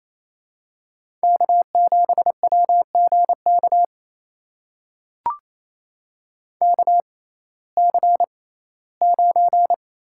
Morse code, heard as a keyed tone, about 28 wpm, spaced as for 17 wpm, 700 Hz. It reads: K7WGK KC9
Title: USA Call Signs - 6x Character Spacing - 28wpm